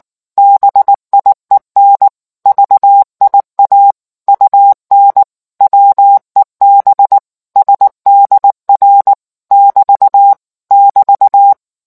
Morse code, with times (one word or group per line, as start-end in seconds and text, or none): 0.4-2.1 BIEN
2.5-3.9 VIA
4.3-5.2 UN
5.6-7.2 WEB
7.6-9.1 SDR
9.5-10.3 <BT>
10.7-11.5 <BT>